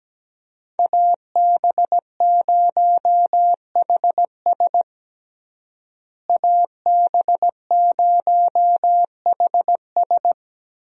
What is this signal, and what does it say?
Morse code, heard as a keyed tone, about 17 wpm, 700 Hz. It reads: AB0HS AB0HS
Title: USA Call Signs - 5 Characters - Repeated 2x - 17wpm